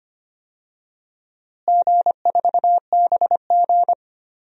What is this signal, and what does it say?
Morse code, heard as a keyed tone, about 25 wpm, 700 Hz. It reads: G4BG
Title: DXCC-100 Call Signs - 4 Characters - Repeated 2x - 25wpm